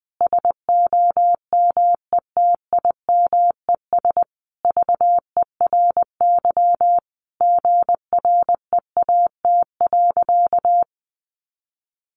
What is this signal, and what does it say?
Morse code, heard as a keyed tone, about 20 wpm, 700 Hz. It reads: SOMETIMES VERY GREAT.
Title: Sentences from Top 1200 Words 20wpm